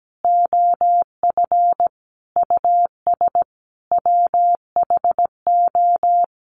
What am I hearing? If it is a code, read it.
Morse code, heard as a keyed tone, about 17 wpm, 700 Hz. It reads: OF US WHO
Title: Sets of 5 Words Encore2 17wpm